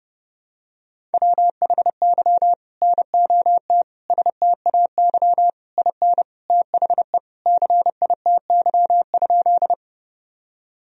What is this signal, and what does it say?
Morse code, heard as a keyed tone, about 30 wpm, 700 Hz. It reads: WHY NOT STAY IN THE CITY?
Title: Sentences from Top 1000 Words - Review Part 1 - 30wpm